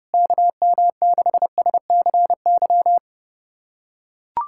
Morse code, heard as a keyed tone, about 32 wpm, 700 Hz. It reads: KM6SCY
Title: USA Callsign DE Callsign 30wpm